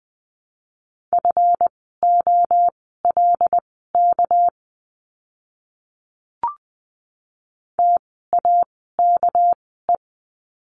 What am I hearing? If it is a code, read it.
Morse code, heard as a keyed tone, about 20 wpm, spaced as for 14 wpm, 700 Hz. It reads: FOLK TAKE
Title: 199 Lesson 32f - Letter k - Words 20wpm